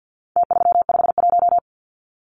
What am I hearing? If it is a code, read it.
Morse code, heard as a keyed tone, about 50 wpm, 700 Hz. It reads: T351
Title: Numbers - Contest Style - 4-Digit 50wpm